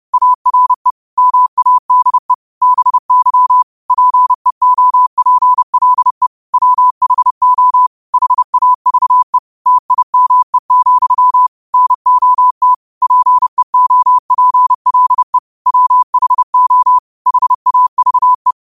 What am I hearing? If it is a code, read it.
Morse code, heard as a keyed tone, about 30 wpm, 1000 Hz. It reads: ARE MADE BY PEOPLE WHO HAVE TIME, NOT PEOPLE WHO HAVE
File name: qotdfiles / 30wpm